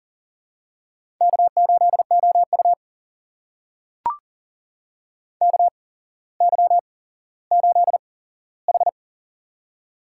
Morse code, heard as a keyed tone, about 39 wpm, spaced as for 17 wpm, 700 Hz. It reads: K8OU KY8H
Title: USA Call Signs - 4 Characters - 8x Character Spacing - 40wpm